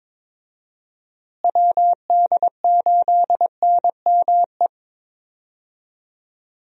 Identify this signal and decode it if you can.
Morse code, heard as a keyed tone, about 22 wpm, 700 Hz. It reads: WD8NME
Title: USA Call Signs - 6 Characters - Repeated 2x - 22wpm